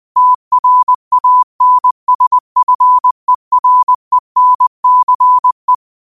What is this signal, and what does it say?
Morse code, heard as a keyed tone, about 20 wpm, 1000 Hz. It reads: TRANSFERENCE